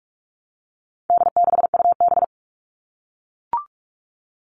Morse code, heard as a keyed tone, about 44 wpm, 700 Hz. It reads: D6UB
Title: All Cut Numbers - 4-Digit 45wpm